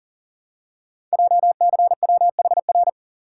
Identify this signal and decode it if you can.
Morse code, heard as a keyed tone, about 42 wpm, 700 Hz. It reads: JCWHR
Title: Code Groups - 5 Letters - 40wpm